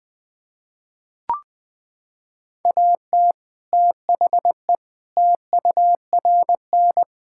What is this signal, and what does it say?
Morse code, heard as a keyed tone, about 20 wpm, 700 Hz. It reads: AT THE TURN
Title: Sentences with 5-Word Phrases - 20wpm